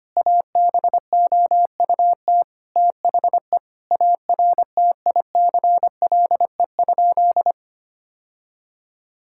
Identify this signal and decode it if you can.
Morse code, heard as a keyed tone, about 25 wpm, 700 Hz. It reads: ABOUT THE ARTICLE?
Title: Sentences from Top 1000 Words - Review Part 5 - 25wpm